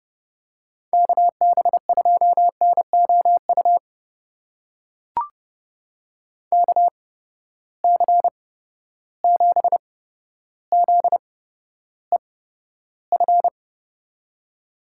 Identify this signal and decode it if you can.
Morse code, heard as a keyed tone, about 31 wpm, spaced as for 13 wpm, 700 Hz. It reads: KB2NOU KC7ZEF
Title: USA Call Signs - 8x Character Spacing - 30wpm